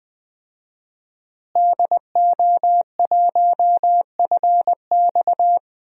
Morse code, heard as a keyed tone, about 20 wpm, 700 Hz. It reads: DO1FX
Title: DXCC-100 Call Signs - 5 Characters - Repeated 3x - 20wpm